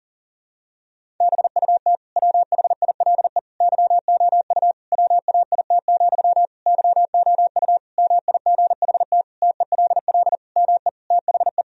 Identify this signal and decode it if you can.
Morse code, heard as a keyed tone, about 37 wpm, 700 Hz. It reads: BUT WHILE YOU WAIT, YOU MIGHT TELL ME THE
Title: Sentences from Top 1000 Words - Review Part 1 - 40wpm